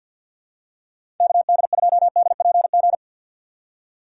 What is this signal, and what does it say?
Morse code, heard as a keyed tone, about 49 wpm, 700 Hz. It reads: KD2DPG
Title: USA Call Signs - 6 Characters - 4x Character Spacing - 50wpm